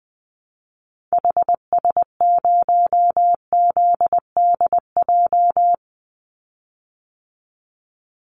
Morse code, heard as a keyed tone, about 20 wpm, 700 Hz. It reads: HS0ZDJ
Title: Call Signs Repeated 3x - 20wpm